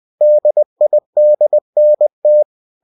Morse code, heard as a keyed tone, about 20 wpm, 600 Hz. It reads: DIDNT